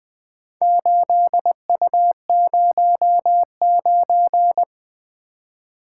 Morse code, heard as a keyed tone, about 20 wpm, 700 Hz. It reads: 8U09
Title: Four Letter-Number Combinations - 20wpm